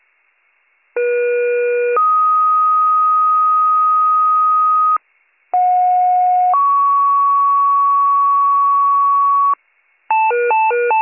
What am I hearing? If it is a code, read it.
Two-tone page, heard as a one-second tone, then a three-second tone, about 480 then 1250 Hz, about 730 then 1120 Hz. It is heard from 1.0 to 9.5 s.